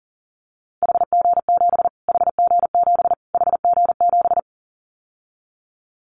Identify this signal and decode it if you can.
Morse code, heard as a keyed tone, about 40 wpm, 700 Hz. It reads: HG7 HG7 HG7
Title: Callsign Prefixes for Top 25 DXCC - Repeated 3x - 40wpm